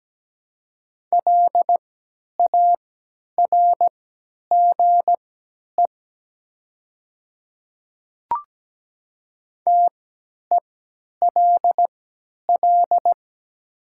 Morse code, heard as a keyed tone, about 17 wpm, spaced as for 10 wpm, 700 Hz. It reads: LARGE TELL
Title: 3x Character Spacing - Top 300 Words - 17wpm